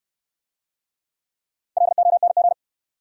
Morse code, heard as a keyed tone, about 70 wpm, 700 Hz. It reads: LONG